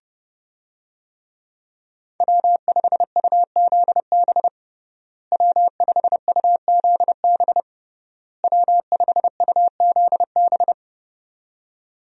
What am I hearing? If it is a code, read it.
Morse code, heard as a keyed tone, about 30 wpm, 700 Hz. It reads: W5UZB W5UZB W5UZB
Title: USA Call Signs - 5 Characters - Repeated 3x - 30wpm